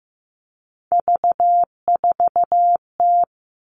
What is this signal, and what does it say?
Morse code, heard as a keyed tone, about 15 wpm, 700 Hz. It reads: V4T
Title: All Cut Numbers - 3-Digit 15wpm